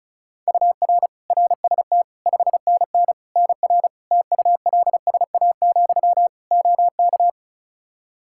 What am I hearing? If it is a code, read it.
Morse code, heard as a keyed tone, about 36 wpm, 700 Hz. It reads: UR RST 5NN NR TULSA, OK
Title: Practice QSO - Exchange 1 - Beginning - 35wpm